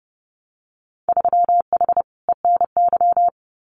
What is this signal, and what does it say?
Morse code, heard as a keyed tone, about 30 wpm, 700 Hz. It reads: 3H ENY
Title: Winter Field Day Exchanges Encore 30wpm